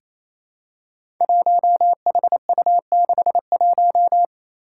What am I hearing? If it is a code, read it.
Morse code, heard as a keyed tone, about 28 wpm, 700 Hz. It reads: 1HU61
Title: Five Challenging Letter-Number Combinations 28wpm